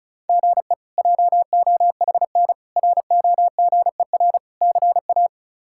Morse code, heard as a keyed tone, about 34 wpm, 700 Hz. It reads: GE JOHN ROGER CA
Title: K1USN SST QSO Practice - 35wpm